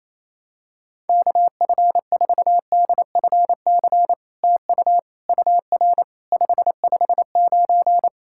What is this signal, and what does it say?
Morse code, heard as a keyed tone, about 28 wpm, 700 Hz. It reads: KF4DFC TU UR 559
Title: SOTA QSO Practice 28wpm